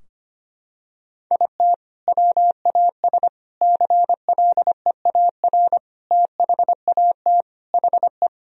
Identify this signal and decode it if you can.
Morse code, heard as a keyed tone, about 25 wpm, 700 Hz. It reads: IT WAS CLEAR THAT HE
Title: Sentences from Top 1000 Words - Review Part 4 - 25wpm